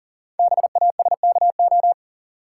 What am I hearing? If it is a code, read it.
Morse code, heard as a keyed tone, about 38 wpm, 690 Hz. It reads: BASKO
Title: Code Groups - 5 Letters - 40wpm